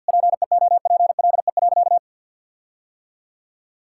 Morse code, heard as a keyed tone, about 50 wpm, 700 Hz. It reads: PEOPLE.
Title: Sentences from Top 1600 Words 50wpm